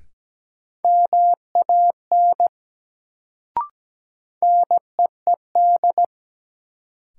Morse code, heard as a keyed tone, about 17 wpm, 700 Hz. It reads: MAN NEED
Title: Sentences from Top 200 Words - New Word Drill - 17wpm